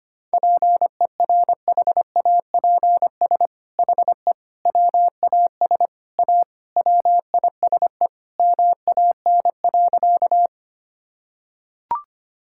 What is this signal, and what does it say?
Morse code, heard as a keyed tone, about 25 wpm, 700 Hz. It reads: PERHAPS HE WAS A WISE MAN.
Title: Sentences from Top 1600 Words 25wpm